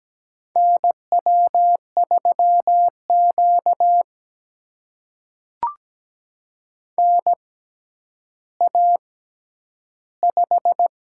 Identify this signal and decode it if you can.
Morse code, heard as a keyed tone, about 17 wpm, spaced as for 10 wpm, 700 Hz. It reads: NW3Q NA5
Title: USA Call Signs - 4 Characters - 6x Character Spacing - 17wpm